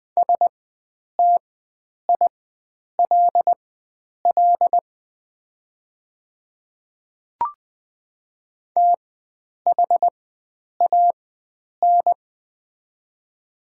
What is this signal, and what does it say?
Morse code, heard as a keyed tone, about 20 wpm, spaced as for 9 wpm, 690 Hz. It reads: STILL THAN
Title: 4x Character Spacing - Top 300 Words - 20wpm